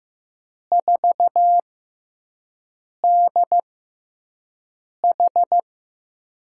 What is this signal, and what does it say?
Morse code, heard as a keyed tone, about 15 wpm, spaced as for 5 wpm, 700 Hz. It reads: 4DH
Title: USA Call Signs - 4 Characters - 6x Character Spacing - 15wpm